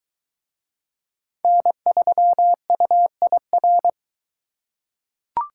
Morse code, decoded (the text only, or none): N3UIR